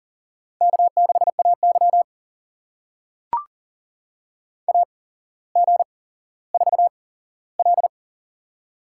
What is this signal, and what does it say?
Morse code, heard as a keyed tone, about 40 wpm, spaced as for 17 wpm, 700 Hz. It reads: K6AY AG4L
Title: USA Call Signs - 8x Character Spacing - 40wpm